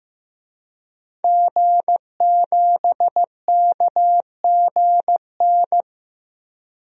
Morse code, heard as a keyed tone, about 15 wpm, 700 Hz. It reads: G7KGN
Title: DXCC-100 Call Signs - 5 Characters - 15wpm